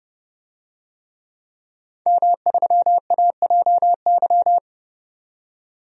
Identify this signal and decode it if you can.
Morse code, heard as a keyed tone, about 30 wpm, 700 Hz. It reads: M3AJY
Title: DXCC-100 Call Signs - 5 Characters - 30wpm